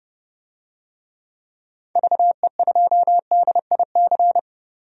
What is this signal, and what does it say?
Morse code, heard as a keyed tone, about 30 wpm, 700 Hz. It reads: VE2DIC
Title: DXCC-100 Call Signs - 6 Characters - 30wpm